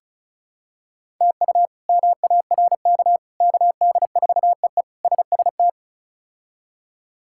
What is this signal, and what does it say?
Morse code, heard as a keyed tone, about 36 wpm, 700 Hz. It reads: TU MARK KD4EE SST
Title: K1USN SST QSO Practice - 35wpm